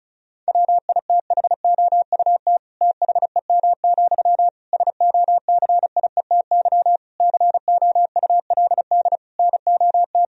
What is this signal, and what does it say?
Morse code, heard as a keyed tone, about 36 wpm, 700 Hz. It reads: WITHOUT THEM, SOCIETY COULD NOT
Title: Sentences from Top 1000 Words - Review Part 4 - 35wpm